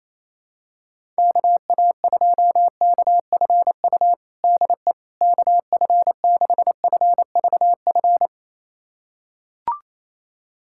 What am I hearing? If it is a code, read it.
Morse code, heard as a keyed tone, about 28 wpm, 700 Hz. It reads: KA2KFU DE KF6FVF
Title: USA Callsign DE Callsign - 6 Characters - 28wpm